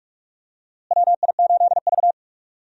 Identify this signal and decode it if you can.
Morse code, heard as a keyed tone, about 46 wpm, 700 Hz. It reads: WI8V
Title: USA Call Signs - 4 Characters - Repeated 3x - 45wpm